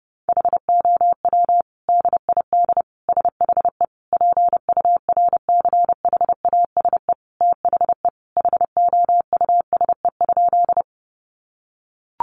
Morse code, heard as a keyed tone, about 30 wpm, 700 Hz. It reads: HOW DID SHE PURCHASE THE HOUSE?
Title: Sentences from Top 800 Words 30wpm